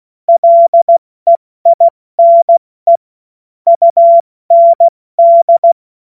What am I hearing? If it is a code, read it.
Morse code, heard as a keyed tone, about 16 wpm, 680 Hz. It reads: LEINE UND
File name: Morsekurs / Witze / Witz 07. Dezember 2021, 70 F80